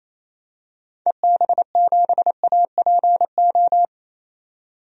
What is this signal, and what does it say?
Morse code, heard as a keyed tone, about 28 wpm, 700 Hz. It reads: EB7APO
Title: DXCC-100 Call Signs - 6 Characters - 28wpm